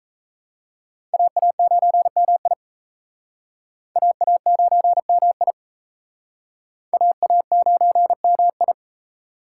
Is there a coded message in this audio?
Yes: Morse code from 1.1 to 8.7 s, heard as a keyed tone, about 38 wpm, 700 Hz. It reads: AA9MI AA9MI AA9MI